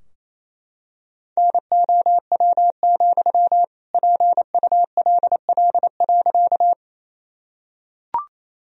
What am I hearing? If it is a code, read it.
Morse code, heard as a keyed tone, about 28 wpm, 700 Hz. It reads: NOW, PULL.
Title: Sentences from Top 1000 Words - Review Part 2 - 28wpm